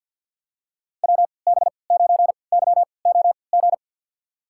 Morse code, heard as a keyed tone, about 47 wpm, spaced as for 28 wpm, 700 Hz. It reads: WB9YOG
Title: USA Call Signs - 6 Characters - 3x Character Spacing - 50wpm